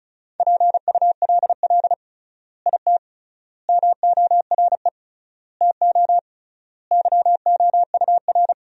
Morse code, heard as a keyed tone, about 34 wpm, 700 Hz. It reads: PULL IT MORE TO YOUR
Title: 3x Word-Spacing - Sentences from Top 400 Words - 35wpm